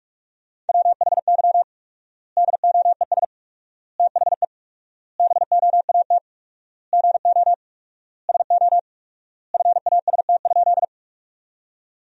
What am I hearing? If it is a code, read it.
Morse code, heard as a keyed tone, about 46 wpm, 700 Hz. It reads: WHY DOES THE BOAT GO SO FAST?
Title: 4x Word-Spacing - Sentences from Top 500 Words - 45wpm